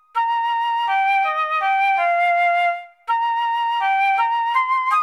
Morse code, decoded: none